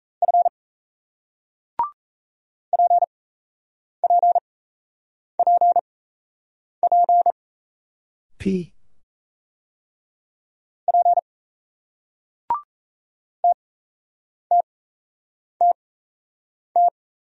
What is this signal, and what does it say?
Morse code, heard as a keyed tone, about 37 wpm, spaced as for 8 wpm, 700 Hz. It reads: F PPPP P TTTT